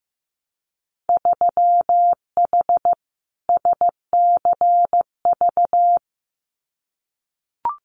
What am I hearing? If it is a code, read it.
Morse code, heard as a keyed tone, about 15 wpm, 700 Hz. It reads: 3H SCV
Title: Winter Field Day Exchanges Encore 15wpm